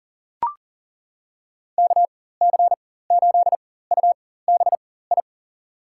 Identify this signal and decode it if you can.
Morse code, heard as a keyed tone, about 40 wpm, spaced as for 19 wpm, 700 Hz. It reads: KC8UBI
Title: USA Call Signs - 4x Character Spacing - 40wpm